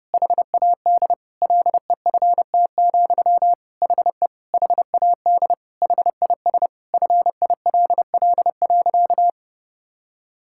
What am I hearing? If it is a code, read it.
Morse code, heard as a keyed tone, about 32 wpm, 700 Hz. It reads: HAD LEFT, HE HAD HIS FILL.
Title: Sentences from Top 1000 Words - Review Part 5 - 30wpm